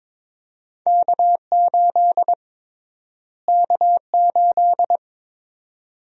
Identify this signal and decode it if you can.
Morse code, heard as a keyed tone, about 21 wpm, 700 Hz. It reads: K8 K8
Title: USA Call Sign Prefixes - Repeated 2x - 22wpm